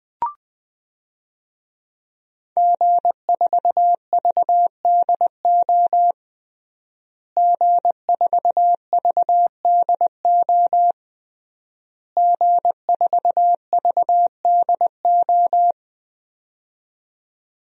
Morse code, heard as a keyed tone, about 20 wpm, 700 Hz. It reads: G4VDO G4VDO G4VDO
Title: DXCC-100 Call Signs - 5 Characters - Repeated 3x - 20wpm